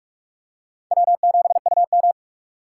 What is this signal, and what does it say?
Morse code, heard as a keyed tone, about 46 wpm, 700 Hz. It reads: W7UM
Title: USA Call Signs - 4 Characters - 4x Character Spacing - 45wpm